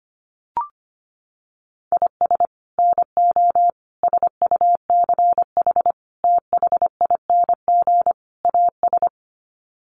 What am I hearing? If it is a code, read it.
Morse code, heard as a keyed tone, about 25 wpm, 700 Hz. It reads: IS NO SUCH THING AS
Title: Sets of 5 Words Encore 25wpm